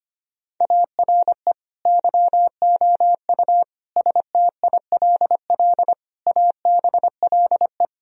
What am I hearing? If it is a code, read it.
Morse code, heard as a keyed tone, about 25 wpm, 700 Hz. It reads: ARE YOU STILL ABLE
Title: Sentences from Top 1000 Words - Review Part 2 - 25wpm